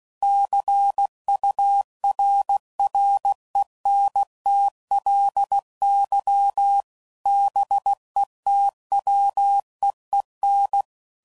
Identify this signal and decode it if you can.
Morse code, heard as a keyed tone, about 16 wpm, 800 Hz. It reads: CURRENTLY BETWEEN
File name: CW Code Groups @15WPM / Weather Sets / Weather 5 22